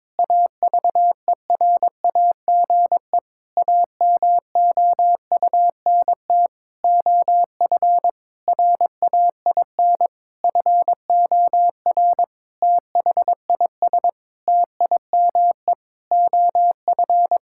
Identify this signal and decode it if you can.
Morse code, heard as a keyed tone, about 22 wpm, 700 Hz. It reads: AVERAGE AMOUNT OF RAIN FOR THIS TIME OF
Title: Sentences from Top 1000 Words - Review Part 6 - 22wpm